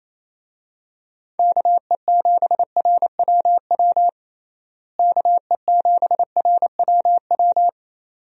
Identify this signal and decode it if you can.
Morse code, heard as a keyed tone, about 27 wpm, 700 Hz. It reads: KE7RWW KE7RWW
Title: USA Call Signs - 6 Characters - Repeated 2x - 28wpm